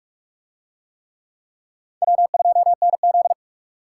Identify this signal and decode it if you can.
Morse code, heard as a keyed tone, about 38 wpm, 700 Hz. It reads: W2NZ